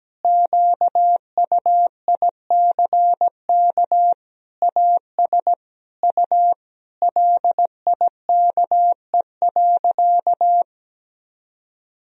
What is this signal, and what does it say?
Morse code, heard as a keyed tone, about 17 wpm, 700 Hz. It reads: QUICK AS U LIKE.
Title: Sentences from Top 500 Words and Abbreviations 17wpm